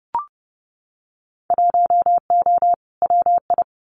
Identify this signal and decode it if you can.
Morse code, heard as a keyed tone, about 32 wpm, 700 Hz. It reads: 1O WI
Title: Winter Field Day Exchanges 30wpm